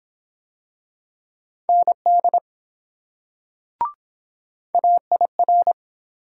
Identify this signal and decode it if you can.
Morse code, heard as a keyed tone, about 23 wpm, 700 Hz. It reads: ND AIR